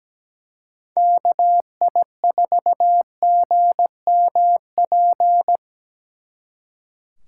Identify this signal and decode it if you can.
Morse code, heard as a keyed tone, about 20 wpm, 700 Hz. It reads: KI4GMP
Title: Speed Racing - USA Call Signs 17wpm